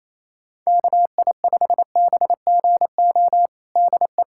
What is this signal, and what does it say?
Morse code, heard as a keyed tone, about 28 wpm, 700 Hz. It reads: KI5BGO DE
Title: USA Callsign DE Callsign 28wpm